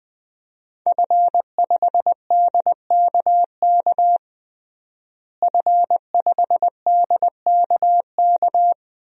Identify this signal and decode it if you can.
Morse code, heard as a keyed tone, about 20 wpm, 700 Hz. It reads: F5DKK F5DKK
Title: DXCC-100 Call Signs - 5 Characters - Repeated 3x - 20wpm